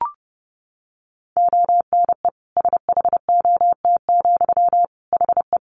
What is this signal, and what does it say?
Morse code, heard as a keyed tone, about 30 wpm, 700 Hz. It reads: ONE SHOT, HE
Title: Sentences from Top 1100 Words 30wpm